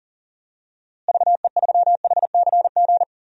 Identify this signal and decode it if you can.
Morse code, heard as a keyed tone, about 40 wpm, 700 Hz. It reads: VE3HCG
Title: DXCC-100 Call Signs - 6 Characters - Repeated 3x - 40wpm